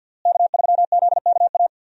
Morse code, heard as a keyed tone, about 52 wpm, 700 Hz. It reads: K3ZKA
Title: USA Call Signs - 5 Characters - Repeated 2x - 50wpm